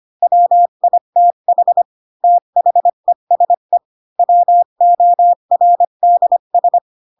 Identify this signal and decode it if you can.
Morse code, heard as a keyed tone, about 25 wpm, 700 Hz. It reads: WITH THESE WORDS
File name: Pinocchio / Pinocchio ch5 20 25 ik1hge